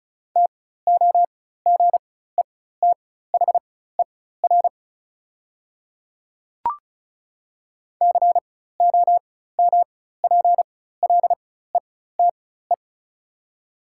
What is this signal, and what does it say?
Morse code, heard as a keyed tone, about 36 wpm, spaced as for 16 wpm, 690 Hz. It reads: TOGETHER COMPLETE
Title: 4x Character Spacing - Top 500 Words - 35wpm